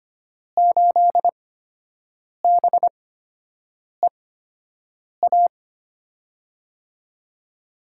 Morse code, heard as a keyed tone, about 25 wpm, spaced as for 7 wpm, 700 Hz. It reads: 8BEA
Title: USA Call Signs - 6 Characters - 8x Character Spacing - 25wpm